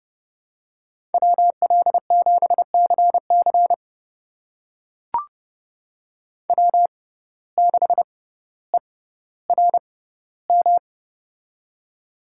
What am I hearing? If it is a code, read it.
Morse code, heard as a keyed tone, about 30 wpm, spaced as for 15 wpm, 700 Hz. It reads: WL7CC W6ERM
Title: USA Call Signs - 5 Characters - 6x Character Spacing - 30wpm